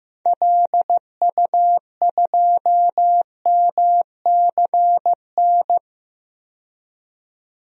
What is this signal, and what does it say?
Morse code, heard as a keyed tone, about 15 wpm, 700 Hz. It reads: LU2MCN
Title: DXCC-100 Call Signs - 6 Characters - 15wpm